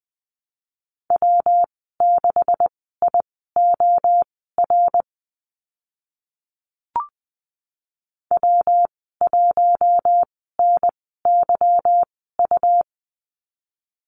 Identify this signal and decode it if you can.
Morse code, heard as a keyed tone, about 20 wpm, spaced as for 15 wpm, 700 Hz. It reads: W6IOR W1NYU
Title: 147 Lesson 24g - Letter y - Call Signs 20wpm